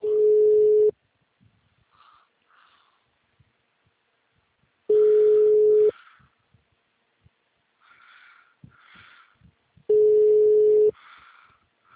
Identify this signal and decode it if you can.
Ringback tone (Europe, 425 Hz).